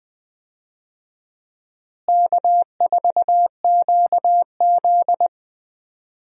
Morse code, heard as a keyed tone, about 20 wpm, 700 Hz. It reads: K4QZ
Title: USA Call Signs - 4 Characters - Repeated 2x - 20wpm